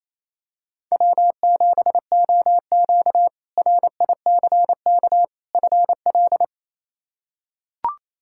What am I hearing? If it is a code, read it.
Morse code, heard as a keyed tone, about 28 wpm, 700 Hz. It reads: W7OQ RICK FL
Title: K1USN SST QSO Practice - 28wpm